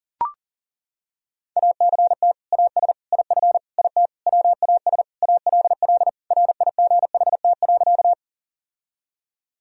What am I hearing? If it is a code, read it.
Morse code, heard as a keyed tone, about 40 wpm, 700 Hz. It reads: ACT AS IF IT WAS ALL RIGHT.
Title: Sentences from Top 1000 Words - Review Part 1 - 40wpm